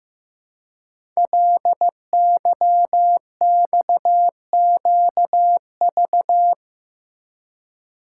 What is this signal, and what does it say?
Morse code, heard as a keyed tone, about 15 wpm, 700 Hz. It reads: LYXQV
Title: Code Groups - 5 Letters - 15wpm